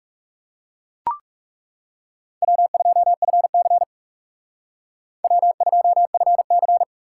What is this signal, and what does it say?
Morse code, heard as a keyed tone, about 34 wpm, 700 Hz. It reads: W2FC W2FC
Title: Speed Racing - USA 1x2 Call Signs 30wpm